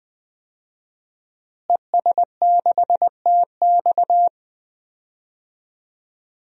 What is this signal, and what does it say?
Morse code, heard as a keyed tone, about 20 wpm, 700 Hz. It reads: ES6TX
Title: DXCC-100 Call Signs - 5 Characters - Repeated 2x - 20wpm